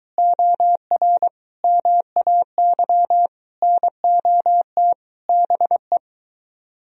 Morse code, heard as a keyed tone, about 23 wpm, 700 Hz. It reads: OR MAY NOT BE